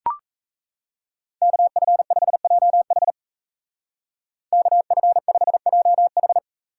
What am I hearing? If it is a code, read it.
Morse code, heard as a keyed tone, about 34 wpm, 700 Hz. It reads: KF5JH KF5JH
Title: Speed Racing - USA 2x2 Call Signs 28wpm